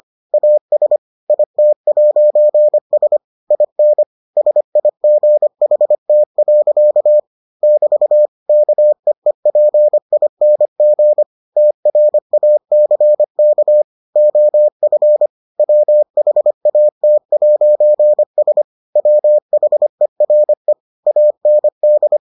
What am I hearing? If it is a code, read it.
Morse code, heard as a keyed tone, about 25 wpm, 600 Hz. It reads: AS IT'S IN SIGHT. <BT> KEEPING TRACK OF WHAT'S WHERE AND